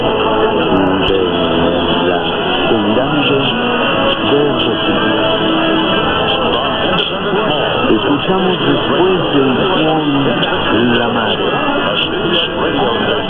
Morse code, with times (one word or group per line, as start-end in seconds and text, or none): none